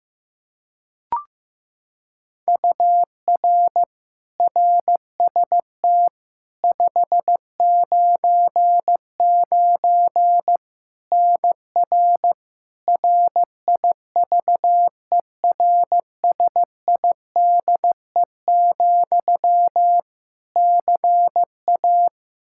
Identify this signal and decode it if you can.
Morse code, heard as a keyed tone, about 15 wpm, 700 Hz. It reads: UR RST 599 NR RIVERSIDE, CA
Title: Practice QSO - Exchange 1 - Beginning - 15wpm